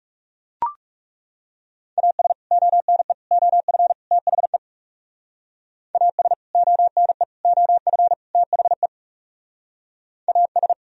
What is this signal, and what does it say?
Morse code, heard as a keyed tone, about 39 wpm, 700 Hz. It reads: AS ONE OF THE AS ONE OF THE AS